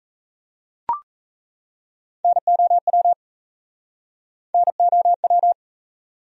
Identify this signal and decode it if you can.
Morse code, heard as a keyed tone, about 37 wpm, 700 Hz. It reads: NOW NOW